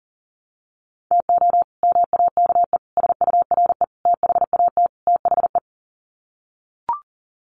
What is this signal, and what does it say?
Morse code, heard as a keyed tone, about 40 wpm, 700 Hz. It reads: TO MAKE SURE THAT THE
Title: Sets of 5 Words 40wpm